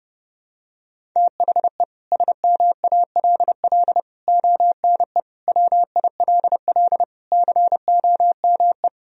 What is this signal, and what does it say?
Morse code, heard as a keyed tone, about 30 wpm, 700 Hz. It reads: THE SMALL ONE WILL COME
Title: Sentences from Top 1000 Words - Review Part 1 - 30wpm